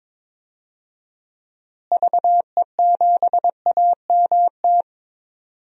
Morse code, heard as a keyed tone, about 21 wpm, 700 Hz. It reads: VE7AMT
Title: DXCC-100 Call Signs - 6 Characters - Repeated 2x - 22wpm